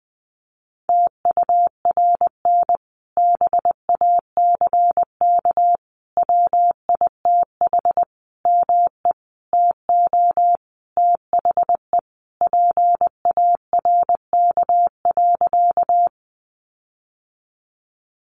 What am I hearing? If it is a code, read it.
Morse code, heard as a keyed tone, about 20 wpm, 700 Hz. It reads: TURN BACK WITH ME TO THE PARK.
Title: Sentences from Top 1300 Words 20wpm